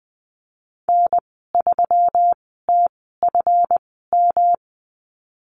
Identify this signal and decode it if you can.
Morse code, heard as a keyed tone, about 20 wpm, spaced as for 15 wpm, 700 Hz. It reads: N3TFM
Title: 154 Lesson 25g - Letter p - Call Signs 20wpm